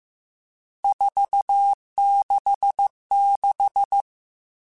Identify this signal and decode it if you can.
Morse code, heard as a keyed tone, about 15 wpm, 800 Hz. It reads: 466
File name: RSTs / RST-mp3s / Various RST @13